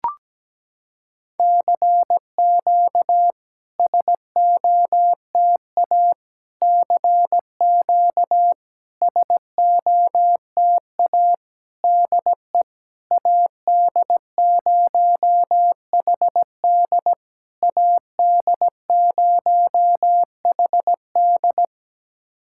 Morse code, heard as a keyed tone, about 17 wpm, 700 Hz. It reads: CQ SOTA CQ SOTA DE AD0HD AD0HD
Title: SOTA QSO Practice 17wpm